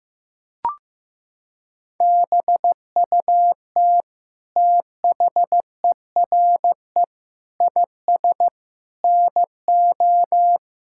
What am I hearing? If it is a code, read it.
Morse code, heard as a keyed tone, about 14 wpm, 700 Hz. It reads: BUT THERE IS NO
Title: Sentences from Top 1000 Words - Review Part 6 - 15wpm